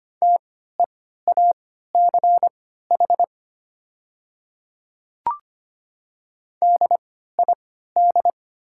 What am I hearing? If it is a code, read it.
Morse code, heard as a keyed tone, about 25 wpm, spaced as for 14 wpm, 700 Hz. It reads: TEACH DID
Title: 3x Character Spacing - Top 500 Words - 25wpm